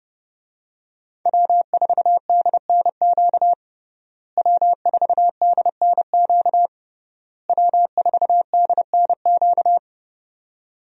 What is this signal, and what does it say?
Morse code, heard as a keyed tone, about 31 wpm, 700 Hz. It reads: W4DNQ W4DNQ W4DNQ
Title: USA Call Signs - 5 Characters - Repeated 3x - 30wpm